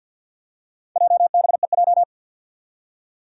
Morse code, heard as a keyed tone, about 52 wpm, 700 Hz. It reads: JBEJ
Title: Four Challenging Letter-Number Combinations 50wpm